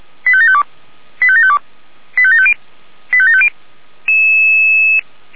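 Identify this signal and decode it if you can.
Five-tone selective call (ZVEI), heard as quick runs of five tones, tones of about 70 ms. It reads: 85752 85752 85750 85750
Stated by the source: ZVEI five-tone selective call received with an SDR, feuerwehr-orlamuende.org fire brigade.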